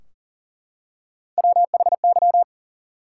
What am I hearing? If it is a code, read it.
Morse code, heard as a keyed tone, about 40 wpm, 700 Hz. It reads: WHY
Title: Three Letter Words - 40wpm